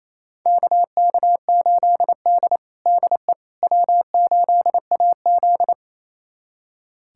Morse code, heard as a keyed tone, about 28 wpm, 700 Hz. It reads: KK8D DE W8AZ
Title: USA Callsign DE Callsign - 4 Characters - 28wpm